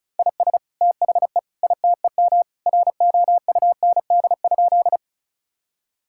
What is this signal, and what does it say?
Morse code, heard as a keyed tone, about 36 wpm, 700 Hz. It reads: IS THE ITEM ROUND?
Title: Sentences from Top 1000 Words - Review Part 6 - 35wpm